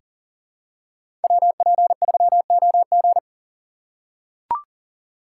Morse code, heard as a keyed tone, about 39 wpm, 700 Hz. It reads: WP3OG
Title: USA Call Signs - 5 Characters - 4x Character Spacing - 40wpm